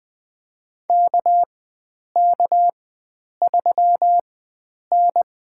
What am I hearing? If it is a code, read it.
Morse code, heard as a keyed tone, about 20 wpm, spaced as for 9 wpm, 700 Hz. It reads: KK3N
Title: USA Call Signs - 4 Characters - 4x Character Spacing - 20wpm